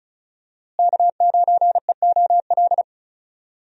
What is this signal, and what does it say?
Morse code, heard as a keyed tone, about 36 wpm, 700 Hz. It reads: K9EOL
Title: USA Call Signs - 5 Characters - Repeated 2x - 35wpm